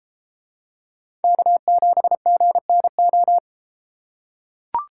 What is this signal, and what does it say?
Morse code, heard as a keyed tone, about 33 wpm, 700 Hz. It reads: K7GNO